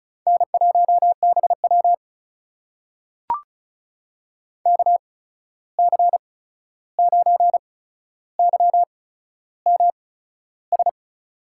Morse code, heard as a keyed tone, about 36 wpm, spaced as for 13 wpm, 700 Hz. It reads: N1BW KC9YMS
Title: USA Call Signs - 8x Character Spacing - 35wpm